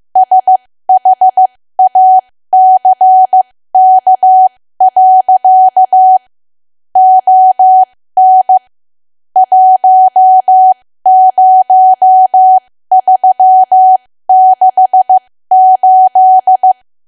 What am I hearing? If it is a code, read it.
Morse code, heard as a keyed tone, about 15 wpm, spaced as for 13 wpm, 750 Hz. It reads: SHACK. ON 10368
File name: cwa / W1AW-Level2 / 13WPM